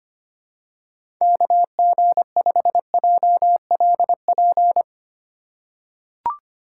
Morse code, heard as a keyed tone, about 25 wpm, 700 Hz. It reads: KG5JLP
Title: USA Call Signs - 3x Character Spacing - 25wpm